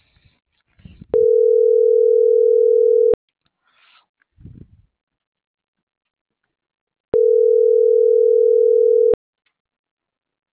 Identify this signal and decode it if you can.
Ringback tone (North America), 440 + 480 Hz.